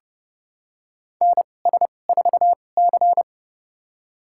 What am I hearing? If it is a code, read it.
Morse code, heard as a keyed tone, about 32 wpm, spaced as for 22 wpm, 700 Hz. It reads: NS4C
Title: USA Call Signs - 4 Characters - 2x Character Spacing - 30wpm